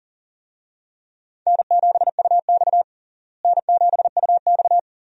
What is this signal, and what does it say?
Morse code, heard as a keyed tone, about 37 wpm, 700 Hz. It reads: N7UX N7UX
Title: USA Call Signs - 4 Characters - Repeated 2x - 40wpm